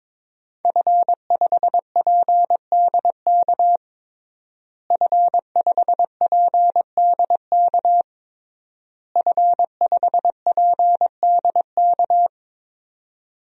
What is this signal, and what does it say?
Morse code, heard as a keyed tone, about 22 wpm, 700 Hz. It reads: F5PDK F5PDK F5PDK
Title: DXCC-100 Call Signs - 5 Characters - Repeated 3x - 22wpm